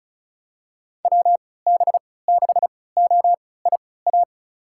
Morse code, heard as a keyed tone, about 36 wpm, spaced as for 20 wpm, 700 Hz. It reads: WB6OIA
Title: USA Call Signs - 6 Characters - 3x Character Spacing - 35wpm